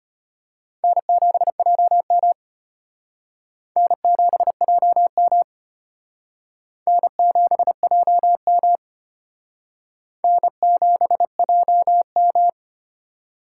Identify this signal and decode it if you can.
Morse code, heard as a keyed tone, about 32 wpm, 700 Hz. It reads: N7JM N7JM N7JM N7JM